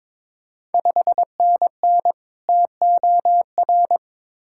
Morse code, heard as a keyed tone, about 22 wpm, 700 Hz. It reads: 5NN TOR
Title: Ontario QSO Party Exchange 22wpm